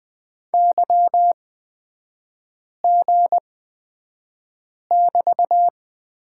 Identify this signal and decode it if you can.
Morse code, heard as a keyed tone, about 20 wpm, spaced as for 5 wpm, 700 Hz. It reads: YG<BT>
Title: 243 Lesson 39e - Letter z - Three Characters 20wpm